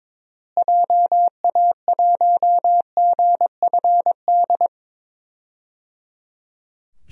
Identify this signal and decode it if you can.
Morse code, heard as a keyed tone, about 22 wpm, 700 Hz. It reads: JA1GFD
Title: DXCC-100 Call Signs - 6 Characters - Repeated 3x - 22wpm